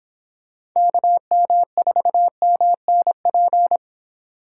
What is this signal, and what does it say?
Morse code, heard as a keyed tone, about 26 wpm, 700 Hz. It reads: KM4MNP
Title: Speed Racing - USA Call Signs 22wpm